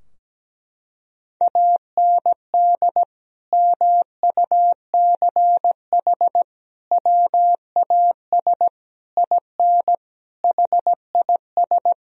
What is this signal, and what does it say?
Morse code, heard as a keyed tone, about 17 wpm, 700 Hz. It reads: AND MUCH WAS IN HIS
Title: Sentences from Top 1000 Words - Review Part 2 - 17wpm